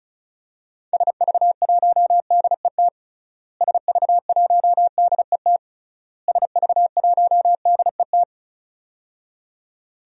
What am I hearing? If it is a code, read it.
Morse code, heard as a keyed tone, about 36 wpm, 700 Hz. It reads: SV1DET SV1DET SV1DET
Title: DXCC-100 Call Signs - 6 Characters - Repeated 3x - 35wpm